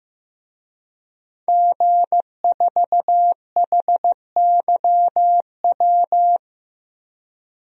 Morse code, heard as a keyed tone, about 15 wpm, 700 Hz. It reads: G4HYW
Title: DXCC-100 Call Signs - 5 Characters - Repeated 3x - 15wpm